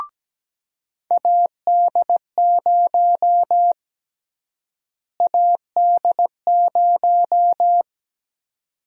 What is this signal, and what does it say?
Morse code, heard as a keyed tone, about 17 wpm, 700 Hz. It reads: AD0 AD0
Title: USA Call Sign Prefixes - Repeated 2x - 17wpm